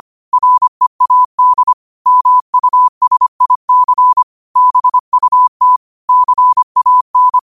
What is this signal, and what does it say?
Morse code, heard as a keyed tone, about 25 wpm, 1000 Hz. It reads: READ MUSIC BUT CAN
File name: qotdfiles / 25wpm